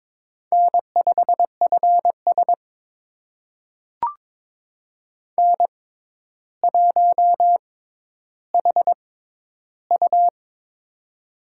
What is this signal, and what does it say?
Morse code, heard as a keyed tone, about 21 wpm, spaced as for 11 wpm, 700 Hz. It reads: N5FS N1HU
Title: USA Call Signs - 4 Characters - 6x Character Spacing - 22wpm